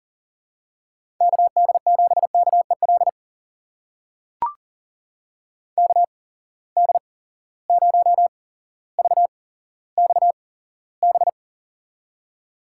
Morse code, heard as a keyed tone, about 40 wpm, spaced as for 17 wpm, 700 Hz. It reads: KD7KEL KD0VXB
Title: USA Call Signs - 6 Characters - 8x Character Spacing - 40wpm